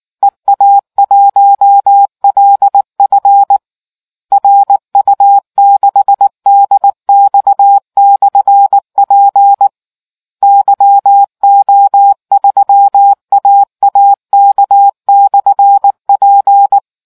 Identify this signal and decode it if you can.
Morse code, heard as a keyed tone, about 19 wpm, 800 Hz. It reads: EA1LF RU6DX/P YO3AAK/P